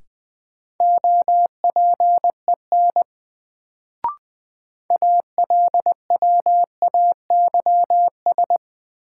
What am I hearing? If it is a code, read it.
Morse code, heard as a keyed tone, about 20 wpm, 700 Hz. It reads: OPEN ALWAYS
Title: Sentences from Top 300 Words - New Word Drill - 20wpm